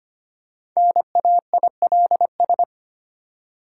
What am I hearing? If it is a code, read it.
Morse code, heard as a keyed tone, about 25 wpm, 690 Hz. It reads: NAILS
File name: practice / 25wpm / Easy Words with Hints - 25wpm